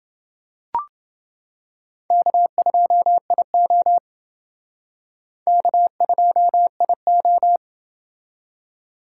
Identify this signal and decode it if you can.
Morse code, heard as a keyed tone, about 28 wpm, 700 Hz. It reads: K2IO K2IO